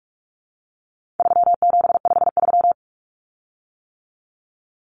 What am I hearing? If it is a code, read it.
Morse code, heard as a keyed tone, about 44 wpm, 700 Hz. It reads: 3753
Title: Common Cut Numbers - 4-Digit 45wpm